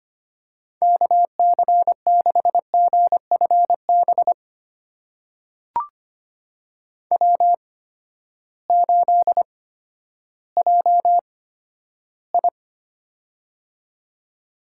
Morse code, heard as a keyed tone, about 25 wpm, spaced as for 12 wpm, 700 Hz. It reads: KC6GFB W8JI
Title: USA Call Signs - 8x Character Spacing - 25wpm